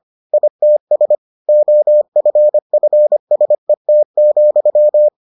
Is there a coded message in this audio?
Yes: Morse code from 0.3 to 5.1 s, heard as a keyed tone, about 25 wpm, 600 Hz. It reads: ITS OFFSET,